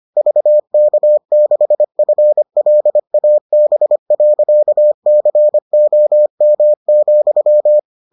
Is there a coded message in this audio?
Yes: Morse code from 0.2 to 7.8 s, heard as a keyed tone, about 25 wpm, 600 Hz. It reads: VK6FLAB.COM,